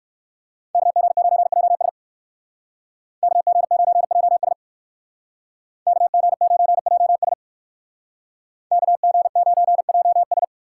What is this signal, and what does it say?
Morse code, heard as a keyed tone, about 59 wpm, 700 Hz. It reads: KG9JS KG9JS KG9JS KG9JS